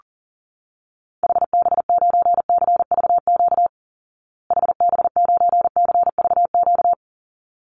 Morse code, heard as a keyed tone, about 40 wpm, 700 Hz. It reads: HB9CVQ HB9CVQ
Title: Call Signs Repeated 3x - 40wpm